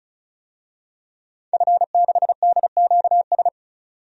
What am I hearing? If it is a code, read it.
Morse code, heard as a keyed tone, about 34 wpm, 700 Hz. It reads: F6DQS
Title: DXCC-100 Call Signs - 5 Characters - 35wpm